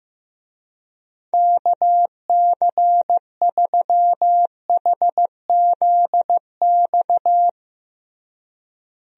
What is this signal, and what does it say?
Morse code, heard as a keyed tone, about 16 wpm, 700 Hz. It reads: KC3HZX
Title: USA Call Signs - 6 Characters - Repeated 3x - 15wpm